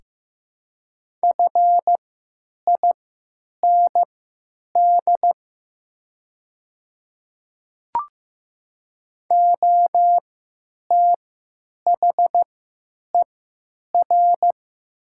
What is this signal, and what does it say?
Morse code, heard as a keyed tone, about 15 wpm, spaced as for 9 wpm, 700 Hz. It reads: FIND OTHER
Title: 3x Character Spacing - Top 100 Words - 15wpm